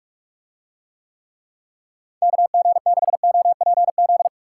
Morse code, heard as a keyed tone, about 44 wpm, 700 Hz. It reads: KG6OPZ